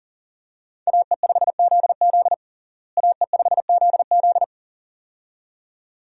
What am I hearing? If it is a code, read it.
Morse code, heard as a keyed tone, about 37 wpm, 700 Hz. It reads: AE5ZZ AE5ZZ
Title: USA Call Signs - 5 Characters - Repeated 2x - 40wpm